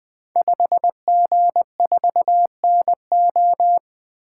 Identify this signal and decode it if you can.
Morse code, heard as a keyed tone, about 20 wpm, 700 Hz. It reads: 5G4NO
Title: Code Groups - 5 Letter:Numbers - 20wpm